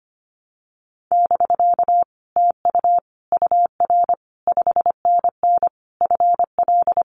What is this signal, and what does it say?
Morse code, heard as a keyed tone, about 25 wpm, 700 Hz. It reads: <BK> TU UR 5NN FL
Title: Common QSO Phrases Encore 25wpm